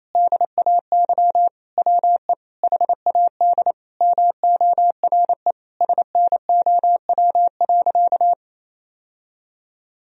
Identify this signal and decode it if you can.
Morse code, heard as a keyed tone, about 27 wpm, 700 Hz. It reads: DAY WE HAD MORE SNOW.
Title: Sentences from Top 1000 Words - Review Part 3 - 28wpm